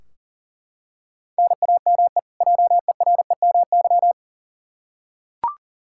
Morse code, heard as a keyed tone, about 40 wpm, 700 Hz. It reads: NAME JEREMY
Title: K1USN SST QSO Practice 40wpm